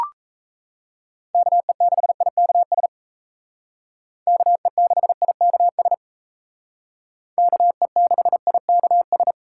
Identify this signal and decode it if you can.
Morse code, heard as a keyed tone, about 35 wpm, 700 Hz. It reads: KE6IKS KE6IKS KE6IKS